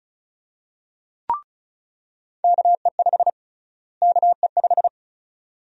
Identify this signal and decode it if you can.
Morse code, heard as a keyed tone, about 36 wpm, 700 Hz. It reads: KE5 KE5
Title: USA Call Sign Prefixes - Repeated 2x - 35wpm